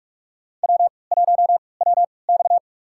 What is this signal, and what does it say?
Morse code, heard as a keyed tone, about 46 wpm, spaced as for 25 wpm, 690 Hz. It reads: W1WX
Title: USA Call Signs - 4 Characters - 3x Character Spacing - 45wpm